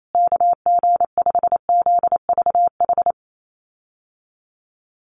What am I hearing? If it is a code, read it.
Morse code, heard as a keyed tone, about 28 wpm, 700 Hz. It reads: KG5ZVH
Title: USA Call Signs Repeated 2x 28wpm